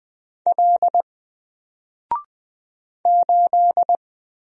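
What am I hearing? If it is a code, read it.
Morse code, heard as a keyed tone, about 20 wpm, 700 Hz. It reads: L 8
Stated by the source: Single letter and number Morse drill at 20 wpm, from N4LCD.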